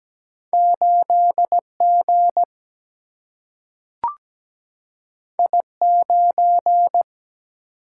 Morse code, heard as a keyed tone, about 17 wpm, 700 Hz. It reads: 8 G I 9
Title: Single-Letter Single-Number 17wpm